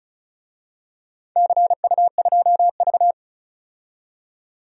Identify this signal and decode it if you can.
Morse code, heard as a keyed tone, about 35 wpm, 700 Hz. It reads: CU2V
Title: DXCC-100 Call Signs - 4 Characters - 35wpm